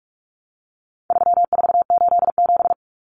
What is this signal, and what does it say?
Morse code, heard as a keyed tone, about 44 wpm, 700 Hz. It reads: 3487
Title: Numbers - 4-Digit 45wpm